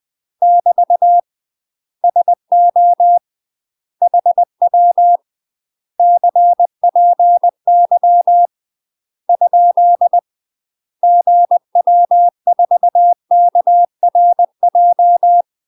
Morse code, heard as a keyed tone, about 20 wpm, 700 Hz. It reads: <BT> SO HW CPY ? GW4KRJ
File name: UK MASTER QSOs TRAINING / 20WPM / UK MASTER QSOs 0001